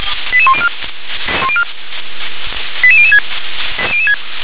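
Five-tone selective call (ZVEI), heard as quick runs of five tones, tones of about 70 ms. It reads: 88104 88006
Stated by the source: Five-tone ZVEI selective call received on an SDR.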